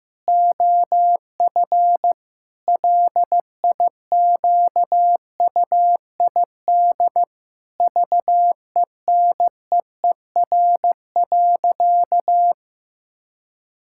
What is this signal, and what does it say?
Morse code, heard as a keyed tone, about 16 wpm, 700 Hz. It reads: OF LIQUID VENEER.
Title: Pangram Challenge v2 - 15wpm